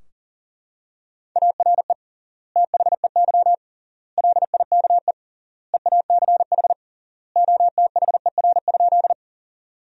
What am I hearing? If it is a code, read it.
Morse code, heard as a keyed tone, about 40 wpm, 700 Hz. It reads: ARE THEY LIKE EACH OTHER?
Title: 3x Word-Spacing - Sentences from Top 200 Words - 40wpm